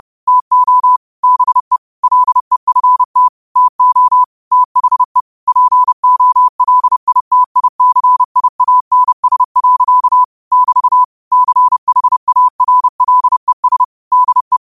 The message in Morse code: TO BE LEFT TO THE POLITICIANS. <BT> CHARLES DE

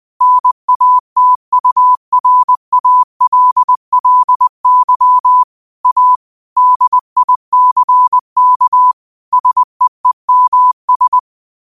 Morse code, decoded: NATURALLY A DICK SEEMS

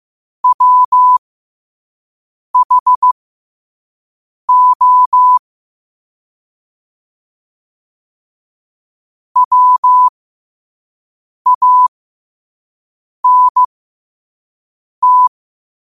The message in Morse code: WHO WANT

